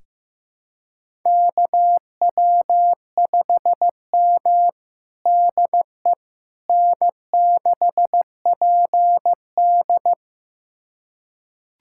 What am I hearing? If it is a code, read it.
Morse code, heard as a keyed tone, about 16 wpm, 700 Hz. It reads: KW5M DE N6PD
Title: USA Callsign DE Callsign - 4 Characters - 15wpm